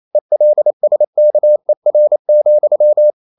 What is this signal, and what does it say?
Morse code, heard as a keyed tone, about 28 wpm, 600 Hz. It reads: ELSKER,
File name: Ferie QRQ 28wpm